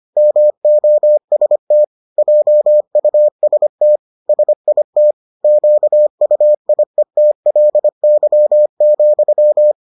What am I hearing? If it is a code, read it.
Morse code, heard as a keyed tone, about 25 wpm, 600 Hz. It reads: MOST JUST SIT QUIETLY,